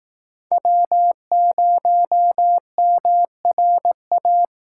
Morse code, heard as a keyed tone, about 18 wpm, 700 Hz. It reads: W0MRA
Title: Speed Racing - USA Call Signs 15wpm